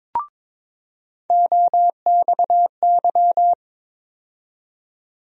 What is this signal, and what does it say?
Morse code, heard as a keyed tone, about 22 wpm, 700 Hz. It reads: OXY
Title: USA Call Sign Postfixes - 3 Letters 22wpm